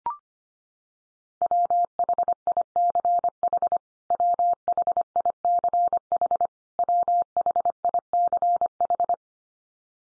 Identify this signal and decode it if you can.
Morse code, heard as a keyed tone, about 25 wpm, 700 Hz. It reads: WHICH WHICH WHICH
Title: Top 100 Words 25wpm x3